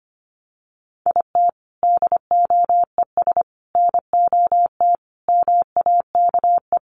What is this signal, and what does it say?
Morse code, heard as a keyed tone, about 25 wpm, 700 Hz. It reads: IT DOES NOT MAKE
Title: Sets of 5 Words Encore2 25wpm